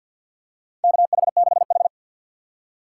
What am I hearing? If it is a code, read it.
Morse code, heard as a keyed tone, about 52 wpm, 700 Hz. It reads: KH6H
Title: USA Call Signs - 4 Characters - 3x Character Spacing - 50wpm